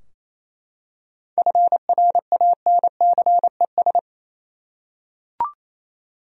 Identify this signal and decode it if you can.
Morse code, heard as a keyed tone, about 28 wpm, 700 Hz. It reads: FRANCES
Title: USA Common Names - 28wpm